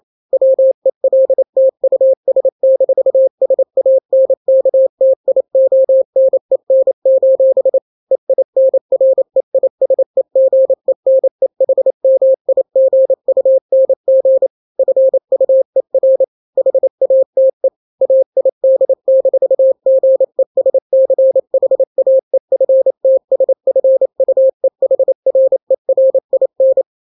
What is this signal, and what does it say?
Morse code, heard as a keyed tone, about 28 wpm, 530 Hz. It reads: WELT US-SANKTIONEN: EINREISEGENEHMIGUNG FUER HATE AID-GESCHAEFTSFUEHRERIN
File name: daily / 27wpm